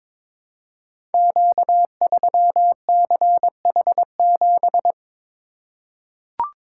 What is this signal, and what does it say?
Morse code, heard as a keyed tone, about 22 wpm, 700 Hz. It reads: Q3CH7
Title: Code Groups - 5 Letter:Numbers - 22wpm